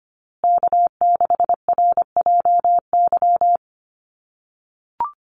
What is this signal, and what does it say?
Morse code, heard as a keyed tone, about 25 wpm, 700 Hz. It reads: K6RJY
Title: USA Call Signs Repeated 2x 25wpm